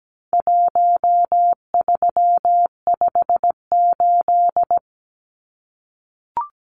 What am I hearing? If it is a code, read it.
Morse code, heard as a keyed tone, about 17 wpm, 700 Hz. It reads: 1358
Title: Numbers - Contest Style - 4-Digit 17wpm